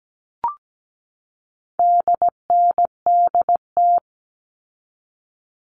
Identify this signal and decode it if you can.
Morse code, heard as a keyed tone, about 17 wpm, 700 Hz. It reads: DNDT
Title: All Cut Numbers - 4-Digit 17wpm